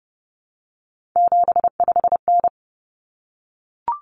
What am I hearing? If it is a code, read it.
Morse code, heard as a keyed tone, about 30 wpm, 700 Hz. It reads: 75N
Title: Common Cut Numbers - 3-Digit 30wpm